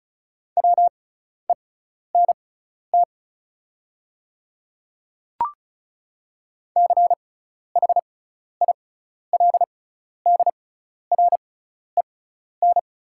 Morse code, heard as a keyed tone, about 36 wpm, spaced as for 12 wpm, 700 Hz. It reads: WENT CHILDREN
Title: 6x Character Spacing - Top 500 Words - 35wpm